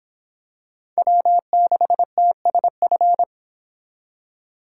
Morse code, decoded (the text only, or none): W6TSF